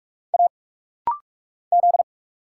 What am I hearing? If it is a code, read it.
Morse code, heard as a keyed tone, about 46 wpm, 700 Hz. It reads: A Z
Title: Single Letters - Rapid-Fire 45wpm